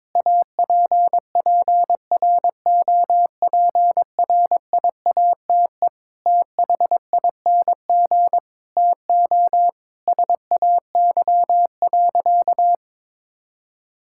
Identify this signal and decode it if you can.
Morse code, heard as a keyed tone, about 22 wpm, 700 Hz. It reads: APPROPRIATE THING TO SAY.
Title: Sentences from Top 1000 Words - Review Part 5 - 22wpm